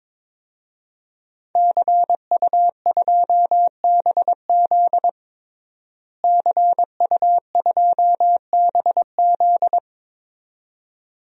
Morse code, heard as a keyed tone, about 22 wpm, 700 Hz. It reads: CU2BZ CU2BZ
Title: DXCC-100 Call Signs - 5 Characters - Repeated 2x - 22wpm